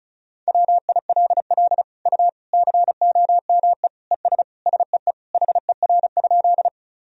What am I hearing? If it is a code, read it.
Morse code, heard as a keyed tone, about 34 wpm, 700 Hz. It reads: WILL U COME ES SEE HER?
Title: Sentences from Top 100 Words and Abbreviations 35wpm